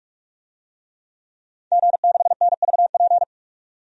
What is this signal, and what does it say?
Morse code, heard as a keyed tone, about 46 wpm, 700 Hz. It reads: G6NVP
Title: DXCC-100 Call Signs - 5 Characters - 45wpm